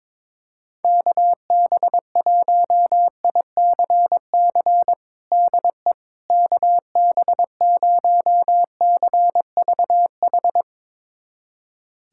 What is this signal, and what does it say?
Morse code, heard as a keyed tone, about 22 wpm, 700 Hz. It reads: KB1ICC DE KB0CVH
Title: USA Callsign DE Callsign 22wpm